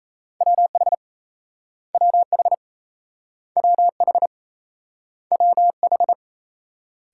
Morse code, heard as a keyed tone, about 38 wpm, 700 Hz. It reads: WH WH WH WH